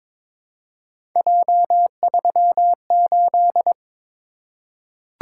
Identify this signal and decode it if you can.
Morse code, heard as a keyed tone, about 22 wpm, 700 Hz. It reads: J38
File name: practice / 22wpm / Three Letter-Number Combinations - 22wpm